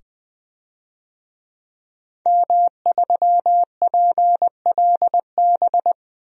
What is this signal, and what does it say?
Morse code, heard as a keyed tone, about 21 wpm, 700 Hz. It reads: M3PLB
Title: DXCC-100 Call Signs - 5 Characters - 20wpm